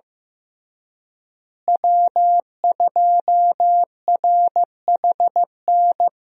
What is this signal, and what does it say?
Morse code, heard as a keyed tone, about 15 wpm, 700 Hz. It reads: W2RHN
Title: USA Call Signs - 5 Characters - Repeated 2x - 15wpm